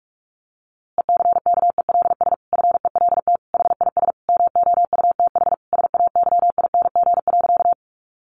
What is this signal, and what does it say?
Morse code, heard as a keyed tone, about 44 wpm, 700 Hz. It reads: ECKELS FELT HIS MOUTH SAYING.
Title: A Sound Of Thunder 45wpm